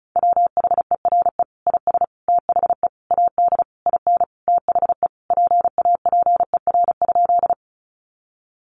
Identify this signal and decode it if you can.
Morse code, heard as a keyed tone, about 34 wpm, 700 Hz. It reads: WHERE IS THE AD IN THE PAPER?
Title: Sentences from Top 1300 Words 35wpm